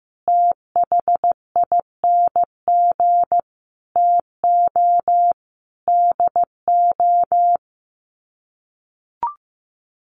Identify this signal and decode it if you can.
Morse code, heard as a keyed tone, about 15 wpm, 700 Hz. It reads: THING TO DO
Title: Sets of 5 Words 15wpm